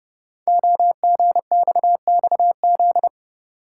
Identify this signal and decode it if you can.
Morse code, heard as a keyed tone, about 30 wpm, 700 Hz. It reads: OGXXZ
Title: Code Groups - 5 Letter:Numbers - 30wpm